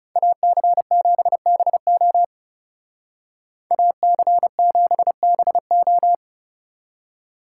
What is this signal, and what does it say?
Morse code, heard as a keyed tone, about 37 wpm, 700 Hz. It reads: AC7BO AC7BO
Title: Speed Racing - USA 2x2 Call Signs 30wpm